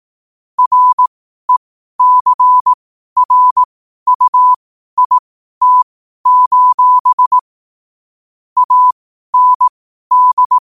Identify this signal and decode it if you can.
Morse code, heard as a keyed tone, about 18 wpm, spaced as for 12 wpm, 1000 Hz. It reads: RECRUIT: AND
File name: qotdfiles / 12wpm